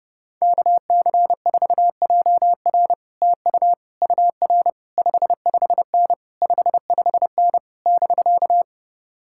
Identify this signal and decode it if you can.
Morse code, heard as a keyed tone, about 32 wpm, 700 Hz. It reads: KC4JR TU UR 55N 55N <BK>